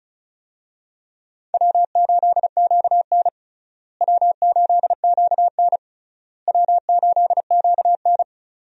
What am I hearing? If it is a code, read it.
Morse code, heard as a keyed tone, about 36 wpm, 700 Hz. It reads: W8QN W8QN W8QN
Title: USA Call Signs - 4 Characters - Repeated 3x - 35wpm